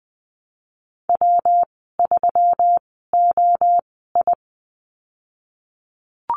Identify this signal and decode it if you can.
Morse code, heard as a keyed tone, about 20 wpm, spaced as for 15 wpm, 700 Hz. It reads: W3OI
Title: 128 Lesson 21g - Number 6 - Call Signs 20wpm